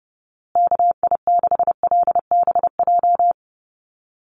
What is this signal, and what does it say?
Morse code, heard as a keyed tone, about 30 wpm, 700 Hz. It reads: KI6LBJ
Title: USA Call Signs Repeated 2x 30wpm